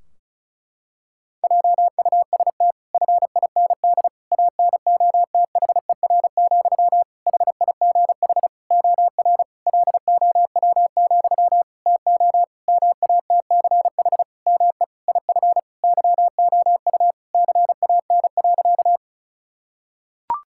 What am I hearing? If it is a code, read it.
Morse code, heard as a keyed tone, about 34 wpm, 700 Hz. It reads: JUST FIND ANOTHER, HIGH OR LOW, TO MATCH ME IF YOU CAN.
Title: Sentences from Top 1600 Words 35wpm